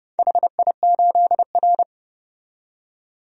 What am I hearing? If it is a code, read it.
Morse code, heard as a keyed tone, about 29 wpm, 700 Hz. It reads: HI8R
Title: DXCC-100 Call Signs - 4 Characters - 30wpm